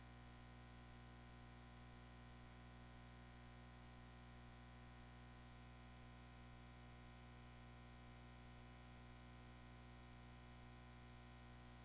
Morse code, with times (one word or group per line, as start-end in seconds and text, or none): none